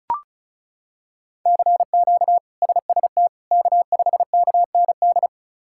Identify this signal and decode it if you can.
Morse code, heard as a keyed tone, about 36 wpm, 700 Hz. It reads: CQ SST K5KND
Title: K1USN SST QSO Practice - 35wpm